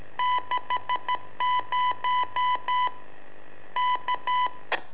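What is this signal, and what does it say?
Morse code, heard as a keyed tone, about 13 wpm, 1000 Hz. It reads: 60 K
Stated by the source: Telemetry in Morse from LU7AA and LU4EZ, AMSAT and Turma Mar del Plata, sent from a balloon.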